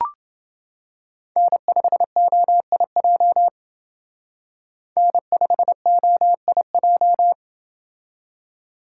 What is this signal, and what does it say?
Morse code, heard as a keyed tone, about 28 wpm, 700 Hz. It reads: N5OIJ N5OIJ